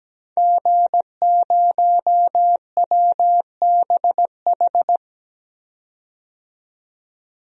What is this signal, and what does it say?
Morse code, heard as a keyed tone, about 17 wpm, 700 Hz. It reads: G0WBH
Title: DXCC-100 Call Signs - 5 Characters - 17wpm